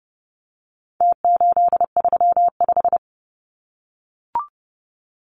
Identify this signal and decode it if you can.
Morse code, heard as a keyed tone, about 30 wpm, 700 Hz. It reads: T835
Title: Numbers - Contest Style - 4-Digit 30wpm